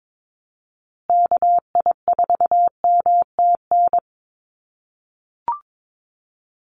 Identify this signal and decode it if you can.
Morse code, heard as a keyed tone, about 22 wpm, 700 Hz. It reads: KI4MTN
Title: USA Call Signs Repeated 2x 22wpm